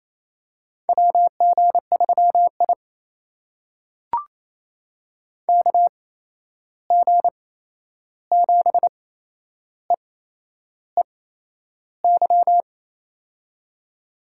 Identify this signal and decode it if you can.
Morse code, heard as a keyed tone, about 28 wpm, spaced as for 11 wpm, 700 Hz. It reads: WG3I KG7EEY
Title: USA Call Signs - 8x Character Spacing - 28wpm